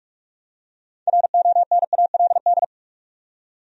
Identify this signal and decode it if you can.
Morse code, heard as a keyed tone, about 46 wpm, 700 Hz. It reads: RONALD
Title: USA Common Names - 45wpm